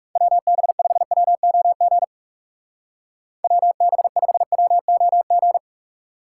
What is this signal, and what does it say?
Morse code, heard as a keyed tone, about 37 wpm, 700 Hz. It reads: WB5WOG WB5WOG
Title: Speed Racing - USA 2x3 Call Signs 30wpm